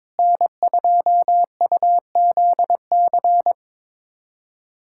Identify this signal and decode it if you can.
Morse code, heard as a keyed tone, about 22 wpm, 700 Hz. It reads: N2UZC
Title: USA Callsign DE Callsign - 5 Characters - 22wpm